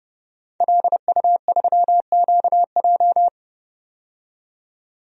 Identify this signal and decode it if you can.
Morse code, heard as a keyed tone, about 29 wpm, 700 Hz. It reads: LU3QJ
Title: DXCC-100 Call Signs - 5 Characters - 30wpm